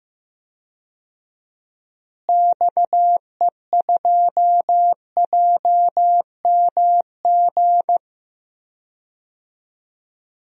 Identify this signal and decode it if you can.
Morse code, heard as a keyed tone, about 15 wpm, 700 Hz. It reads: XE2JMG
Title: DXCC-100 Call Signs - 6 Characters - Repeated 2x - 15wpm